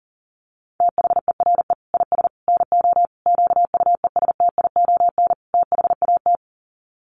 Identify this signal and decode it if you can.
Morse code, heard as a keyed tone, about 40 wpm, 700 Hz. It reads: THERE IS NO QUESTION THAT
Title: Sets of 5 Words Encore 40wpm